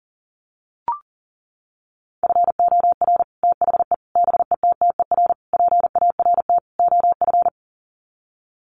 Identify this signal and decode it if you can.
Morse code, heard as a keyed tone, about 40 wpm, 700 Hz. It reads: FOR THE BETTER PART OF
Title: Sets of 5 Words Encore2 40wpm